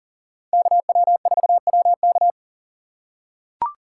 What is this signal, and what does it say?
Morse code, heard as a keyed tone, about 40 wpm, 700 Hz. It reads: KW4WK
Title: USA Call Signs - 5 Characters - 2x Character Spacing - 40wpm